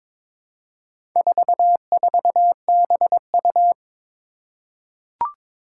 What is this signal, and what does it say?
Morse code, heard as a keyed tone, about 22 wpm, 700 Hz. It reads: 44BU
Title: Four Challenging Letter-Number Combinations 22wpm